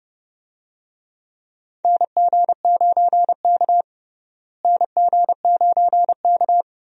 Morse code, heard as a keyed tone, about 30 wpm, 700 Hz. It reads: NG9K NG9K
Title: USA Call Signs - 4 Characters - Repeated 2x - 30wpm